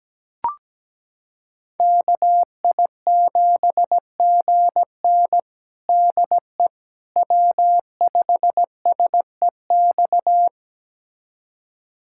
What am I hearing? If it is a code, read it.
Morse code, heard as a keyed tone, about 17 wpm, 700 Hz. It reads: KI7GN DE W5SEX
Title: USA Callsign DE Callsign - 5 Characters - 17wpm